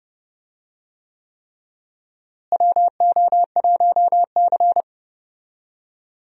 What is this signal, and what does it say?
Morse code, heard as a keyed tone, about 30 wpm, 700 Hz. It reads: WO1C
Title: USA Call Signs - 4 Characters - Repeated 3x - 30wpm